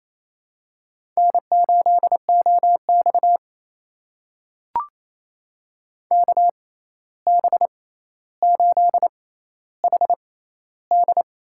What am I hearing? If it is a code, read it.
Morse code, heard as a keyed tone, about 28 wpm, spaced as for 13 wpm, 700 Hz. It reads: N8OX KB8HD
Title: USA Call Signs - 6x Character Spacing - 28wpm